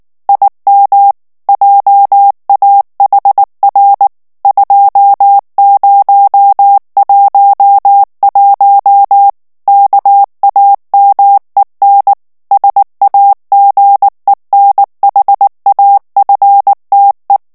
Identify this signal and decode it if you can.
Morse code, heard as a keyed tone, about 19 wpm, 800 Hz. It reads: IM JAHR 2011 KAMEN SAGENHAFTE
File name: morse mp3 / ALLEMAND texte clair 2 vitesse 18 mots minute